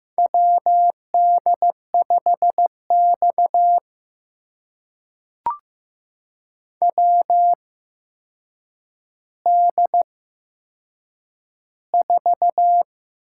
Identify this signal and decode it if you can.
Morse code, heard as a keyed tone, about 15 wpm, spaced as for 7 wpm, 700 Hz. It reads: WD5X WD4
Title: USA Call Signs - 4 Characters - 8x Character Spacing - 15wpm